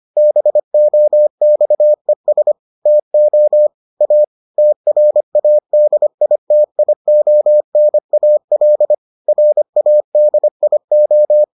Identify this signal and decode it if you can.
Morse code, heard as a keyed tone, about 25 wpm, 600 Hz. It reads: BOXES TO A TRADITIONAL RADIO